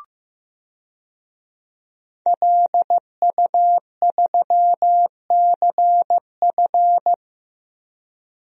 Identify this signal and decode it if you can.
Morse code, heard as a keyed tone, about 15 wpm, 700 Hz. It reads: LU3CF
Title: DXCC-100 Call Signs - 5 Characters - Repeated 3x - 15wpm